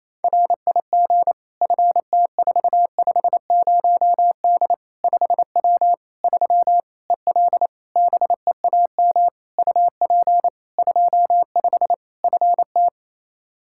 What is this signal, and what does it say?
Morse code, heard as a keyed tone, about 27 wpm, 700 Hz. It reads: RIG FT450D 5W 3 EL BEAM UP 25 FT
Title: Practice QSO - Exchange 2 - Info - 28wpm